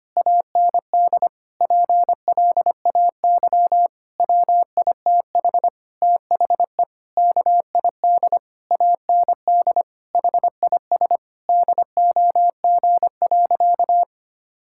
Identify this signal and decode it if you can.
Morse code, heard as a keyed tone, about 25 wpm, 700 Hz. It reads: AND PLAY WITH THE KID AND HIS DOG.
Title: Sentences from Top 1600 Words 25wpm